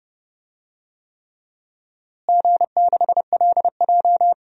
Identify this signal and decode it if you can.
Morse code, heard as a keyed tone, about 30 wpm, 700 Hz. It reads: G6LJ
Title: DXCC-100 Call Signs - 4 Characters - Repeated 3x - 30wpm